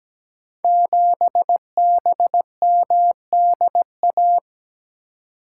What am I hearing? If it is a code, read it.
Morse code, heard as a keyed tone, about 17 wpm, 700 Hz. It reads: 7BMDA
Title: Code Groups - 5 Letter:Numbers - 17wpm